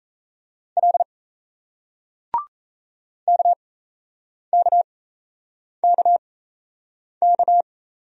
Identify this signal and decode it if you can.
Morse code, heard as a keyed tone, about 38 wpm, spaced as for 8 wpm, 700 Hz. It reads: L KKKK